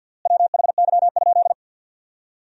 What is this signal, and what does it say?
Morse code, heard as a keyed tone, about 50 wpm, 700 Hz. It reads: WHY?
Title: 3x Word-Spacing - Sentences from Top 200 Words - 50wpm